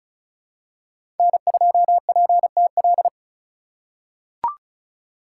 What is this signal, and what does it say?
Morse code, heard as a keyed tone, about 36 wpm, 700 Hz. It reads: N2PTL
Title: USA Call Signs - 4x Character Spacing - 35wpm